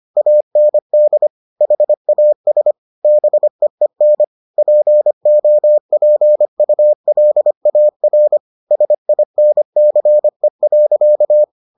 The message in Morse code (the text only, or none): AND HAS BEEN POPULAR SINCE.